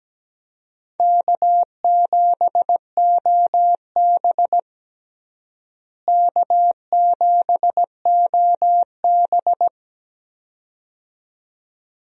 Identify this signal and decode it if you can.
Morse code, heard as a keyed tone, about 17 wpm, 700 Hz. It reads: K7OB K7OB
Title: USA Call Signs - 4 Characters - Repeated 2x - 17wpm